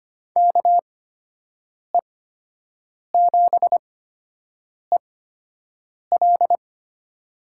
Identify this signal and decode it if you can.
Morse code, heard as a keyed tone, about 25 wpm, spaced as for 7 wpm, 700 Hz. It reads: KE7EL
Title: USA Call Signs - 5 Characters - 8x Character Spacing - 25wpm